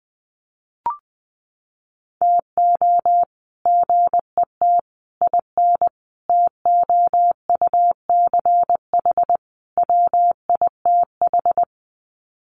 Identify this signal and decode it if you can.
Morse code, heard as a keyed tone, about 20 wpm, 700 Hz. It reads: TO GET IN TOUCH WITH
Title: Sets of 5 Words Encore2 20wpm